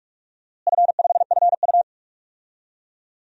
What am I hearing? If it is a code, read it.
Morse code, heard as a keyed tone, about 46 wpm, 700 Hz. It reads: F5FU
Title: DXCC-100 Call Signs - 4 Characters - Repeated 2x - 45wpm